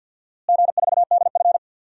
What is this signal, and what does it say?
Morse code, heard as a keyed tone, about 52 wpm, 700 Hz. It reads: G4DF
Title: DXCC-100 Call Signs - 4 Characters - Repeated 2x - 50wpm